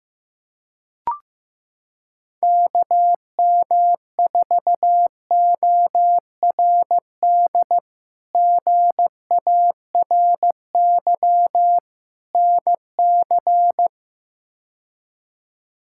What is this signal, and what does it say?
Morse code, heard as a keyed tone, about 16 wpm, 700 Hz. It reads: KM4ORD GARY NC
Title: K1USN SST QSO Practice - 15wpm